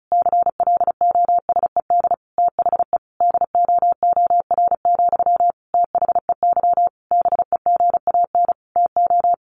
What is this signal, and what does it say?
Morse code, heard as a keyed tone, about 34 wpm, 700 Hz. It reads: CLOSED THE DOOR, THEY BEGAN TO
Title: Sentences from Top 1400 Words 35wpm